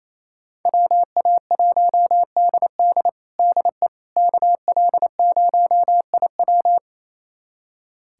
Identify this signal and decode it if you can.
Morse code, heard as a keyed tone, about 28 wpm, 700 Hz. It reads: WA1DD DE KL0IW
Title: USA Callsign DE Callsign - 5 Characters - 28wpm